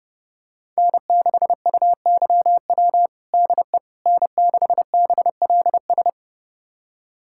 Morse code, heard as a keyed tone, about 30 wpm, 700 Hz. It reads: N6UYW DE N6BLS
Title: USA Callsign DE Callsign - 5 Characters - 30wpm